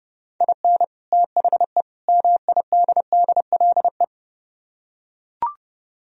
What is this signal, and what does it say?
Morse code, heard as a keyed tone, about 30 wpm, 700 Hz. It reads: IN THE MIDDLE